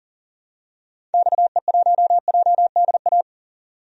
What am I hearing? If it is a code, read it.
Morse code, heard as a keyed tone, about 37 wpm, 700 Hz. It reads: XE1JDA
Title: DXCC-100 Call Signs - 6 Characters - 40wpm